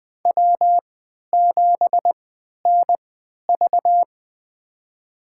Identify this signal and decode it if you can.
Morse code, heard as a keyed tone, about 21 wpm, spaced as for 12 wpm, 700 Hz. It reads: W7NV
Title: USA Call Signs - 3x Character Spacing - 20wpm